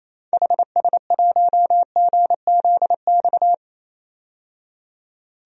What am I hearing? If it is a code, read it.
Morse code, heard as a keyed tone, about 28 wpm, 700 Hz. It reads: HS1GZX
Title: DXCC-100 Call Signs - 6 Characters - 28wpm